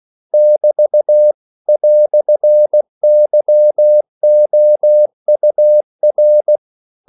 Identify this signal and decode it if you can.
Morse code, heard as a keyed tone, about 16 wpm, 600 Hz. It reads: <BT> "YOUR
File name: CW / Practice / EnglishQuotations0416